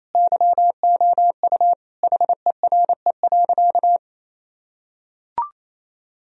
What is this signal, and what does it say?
Morse code, heard as a keyed tone, about 28 wpm, 700 Hz. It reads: YOU HERE.
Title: Sentences from Top 1000 Words - Review Part 2 - 28wpm